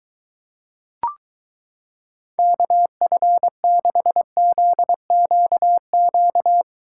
Morse code, heard as a keyed tone, about 23 wpm, 700 Hz. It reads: KF6ZQQ